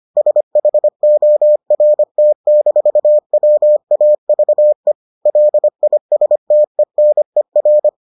Morse code, heard as a keyed tone, about 25 wpm, 600 Hz. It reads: SHORT-WAVE LISTENER